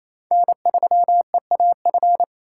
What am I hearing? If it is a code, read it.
Morse code, heard as a keyed tone, about 28 wpm, 700 Hz. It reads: N3EAF
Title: USA Call Signs - 5 Characters - 2x Character Spacing - 28wpm